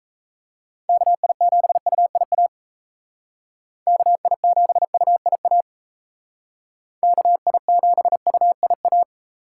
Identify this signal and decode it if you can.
Morse code, heard as a keyed tone, about 37 wpm, 700 Hz. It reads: KI7UIA KI7UIA KI7UIA